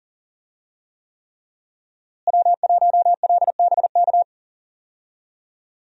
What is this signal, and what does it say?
Morse code, heard as a keyed tone, about 40 wpm, 700 Hz. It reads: W1LBK
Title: USA Call Signs - 5 Characters - Repeated 3x - 40wpm